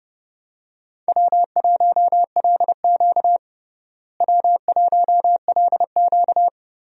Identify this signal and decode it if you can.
Morse code, heard as a keyed tone, about 30 wpm, 700 Hz. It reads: W1LQ W1LQ
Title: USA Call Signs - 4 Characters - Repeated 3x - 30wpm